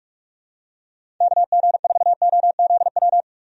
Morse code, heard as a keyed tone, about 46 wpm, 700 Hz. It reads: KG4OZW
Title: USA Call Signs - 6 Characters - 45wpm